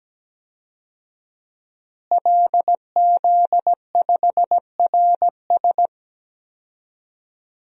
Morse code, heard as a keyed tone, about 17 wpm, 700 Hz. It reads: LZ5RS
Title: DXCC-100 Call Signs - 5 Characters - Repeated 2x - 17wpm